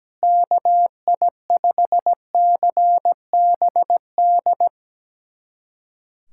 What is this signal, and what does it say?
Morse code, heard as a keyed tone, about 17 wpm, 700 Hz. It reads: KI5CBD
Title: USA Callsign DE Callsign - 6 Characters - 17wpm